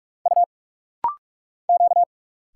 Morse code, heard as a keyed tone, about 46 wpm, 700 Hz. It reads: U Q
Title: Single Letters - Rapid-Fire 45wpm